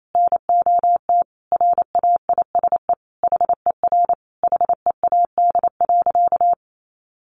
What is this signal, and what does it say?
Morse code, heard as a keyed tone, about 28 wpm, 700 Hz. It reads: NOT RAISE HER HEAD.
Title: Sentences from Top 1400 Words 28wpm